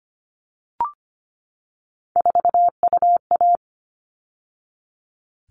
Morse code, heard as a keyed tone, about 25 wpm, 700 Hz. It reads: 4UA
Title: All Cut Numbers - 3-Digit 25wpm